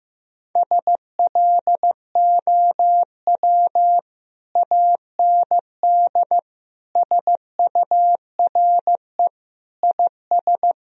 Morse code, heard as a keyed tone, about 15 wpm, 700 Hz. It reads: SLOW AND SURE IS
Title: Sentences from Top 1000 Words - Review Part 2 - 15wpm